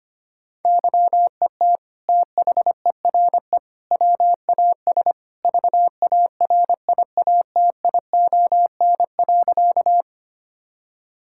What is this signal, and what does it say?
Morse code, heard as a keyed tone, about 25 wpm, 700 Hz. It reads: YET THERE WAS VARIATION.
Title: Sentences from Top 1600 Words 25wpm